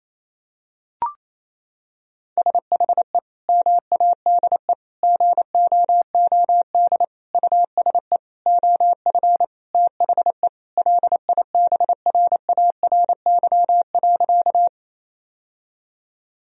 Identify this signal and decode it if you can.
Morse code, heard as a keyed tone, about 28 wpm, 700 Hz. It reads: SHE MADE GOOD USE OF THE LIBRARY.
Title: Sentences from Top 1000 Words - Review Part 4 - 28wpm